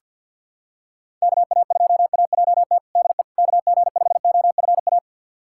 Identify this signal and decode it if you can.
Morse code, heard as a keyed tone, about 50 wpm, 700 Hz. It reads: KA2AJT DE KG5OFA
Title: USA Callsign DE Callsign - 6 Characters - 50wpm